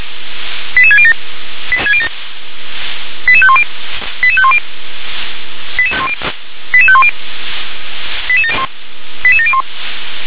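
Five-tone selective call (ZVEI), heard as quick runs of five tones, tones of about 70 ms. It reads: 88607 88510 88510 88510 88701
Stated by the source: Five-tone ZVEI selective call 88720, received by SDR.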